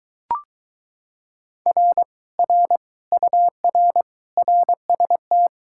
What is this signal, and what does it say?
Morse code, heard as a keyed tone, about 23 wpm, 700 Hz. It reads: R R UR RST